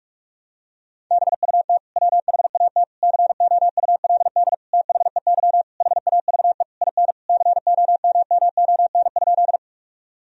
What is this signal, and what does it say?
Morse code, heard as a keyed tone, about 47 wpm, 700 Hz. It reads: BUT WHAT COULD THEY HAVE IN COMMON?
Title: Sentences from Top 1000 Words - Review Part 3 - 45wpm